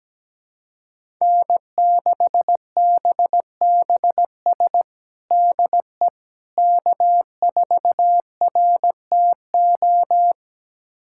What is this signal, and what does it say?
Morse code, heard as a keyed tone, about 17 wpm, 700 Hz. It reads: N6BBS DE K4RTO
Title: USA Callsign DE Callsign - 5 Characters - 17wpm